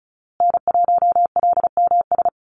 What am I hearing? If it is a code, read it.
Morse code, heard as a keyed tone, about 34 wpm, 700 Hz. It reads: N1LMS
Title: USA Call Signs Repeated 2x 35wpm